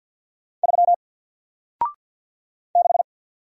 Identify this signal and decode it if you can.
Morse code, heard as a keyed tone, about 50 wpm, 700 Hz. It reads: 3 6
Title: Single Letter-Number 50wpm